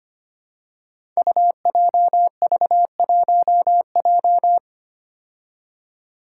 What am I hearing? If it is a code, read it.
Morse code, heard as a keyed tone, about 25 wpm, 700 Hz. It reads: UJV1J
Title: Five Challenging Letter-Number Combinations 25wpm